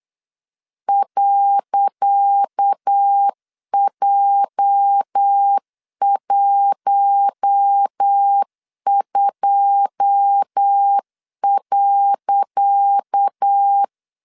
Morse code, heard as a keyed tone, about 8 wpm, 800 Hz. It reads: .J12.
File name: LEZIONI / 10 SIGNS / 10C DOT 8wpm 3649